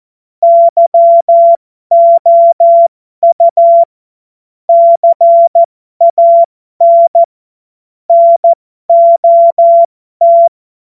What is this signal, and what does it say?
Morse code, heard as a keyed tone, about 14 wpm, spaced as for 12 wpm, 680 Hz. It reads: YOU CAN NOT
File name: Morsekurs / Witze